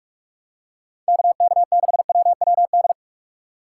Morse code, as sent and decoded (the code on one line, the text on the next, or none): -.- -.- -.... .-- .-- -..
KK6WWD